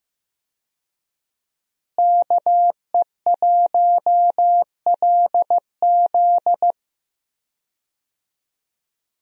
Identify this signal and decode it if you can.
Morse code, heard as a keyed tone, about 15 wpm, 700 Hz. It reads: KE1LZ
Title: USA Call Signs - 5 Characters - Repeated 2x - 15wpm